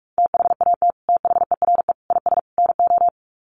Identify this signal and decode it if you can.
Morse code, heard as a keyed tone, about 44 wpm, 700 Hz. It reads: THAT THERE IS NO
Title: Sets of 4 Words 45wpm